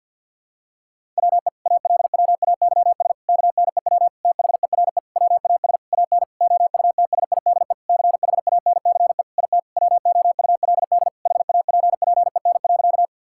Morse code, heard as a keyed tone, about 51 wpm, 700 Hz. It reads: WE ALWAYS KNEW THERE WAS AN OUTSIDE CHANCE IT WOULD HAPPEN.
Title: Sentences from Top 1000 Words - Review Part 6 - 50wpm